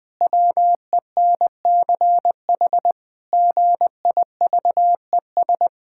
Morse code, heard as a keyed tone, about 19 wpm, 700 Hz. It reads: WENCH GIVES
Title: Pangram Challenge v2 - 20wpm